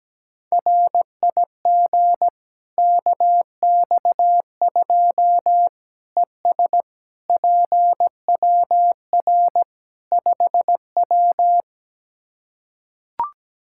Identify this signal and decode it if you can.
Morse code, heard as a keyed tone, about 17 wpm, 700 Hz. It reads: RIG KX2 ES PWR 5W
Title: K1USN SST QSO Practice 17wpm